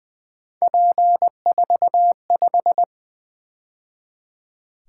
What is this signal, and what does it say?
Morse code, heard as a keyed tone, about 24 wpm, 700 Hz. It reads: P45
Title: Speed-Racing - Three Letter-Number Combinations - 20wpm